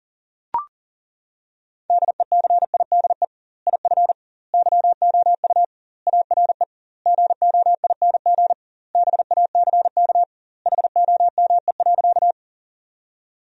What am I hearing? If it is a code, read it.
Morse code, heard as a keyed tone, about 40 wpm, 700 Hz. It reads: DECIDE IF YOU ARE GOING BACK HOME.
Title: 2x Word-Spacing - Sentences from Top 500 Words - 40wpm